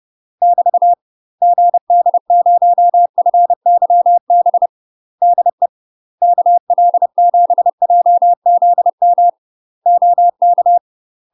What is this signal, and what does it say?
Morse code, heard as a keyed tone, about 30 wpm, 700 Hz. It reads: <BT> GD0FYB DE KL7JZM OK
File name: UK MASTER QSOs TRAINING/30WPM